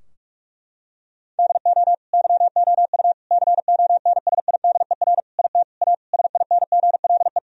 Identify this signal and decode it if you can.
Morse code, heard as a keyed tone, about 41 wpm, 700 Hz. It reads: DO YOU CONSIDER IT A SINGLE
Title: Sentences from Top 1000 Words - Review Part 4 - 45wpm